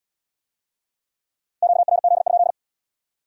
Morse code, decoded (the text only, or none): QRZ?